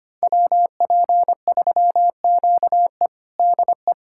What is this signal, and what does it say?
Morse code, heard as a keyed tone, about 25 wpm, 700 Hz. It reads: WP3QE DE
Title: USA Callsign DE Callsign - 5 Characters - 25wpm